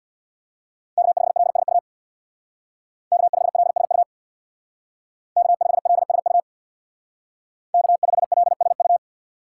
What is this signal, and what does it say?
Morse code, heard as a keyed tone, about 63 wpm, 700 Hz. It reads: K5LSU K5LSU K5LSU K5LSU